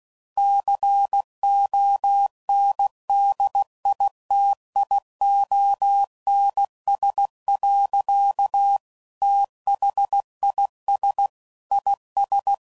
Morse code, heard as a keyed tone, about 16 wpm, 800 Hz. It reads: CONDITIONS. THIS IS
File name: CW Code Groups @15WPM / Technical Subjects / AND Gate